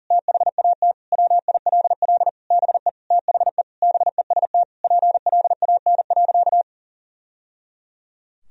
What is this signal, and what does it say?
Morse code, heard as a keyed tone, about 40 wpm, 700 Hz. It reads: THAT WILL BE THE BEST PLAN.
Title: Sentences from Top 1000 Words - Review Part 2 - 40wpm